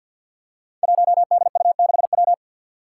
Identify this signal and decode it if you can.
Morse code, heard as a keyed tone, about 53 wpm, 700 Hz. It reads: 1DU6W
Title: Five Challenging Letter-Number Combinations 50wpm